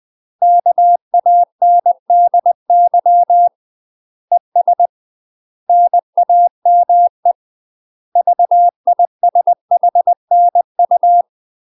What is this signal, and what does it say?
Morse code, heard as a keyed tone, about 20 wpm, 700 Hz. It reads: KANDY ES NAME VISHNU
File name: UK MASTER QSOs TRAINING/20WPM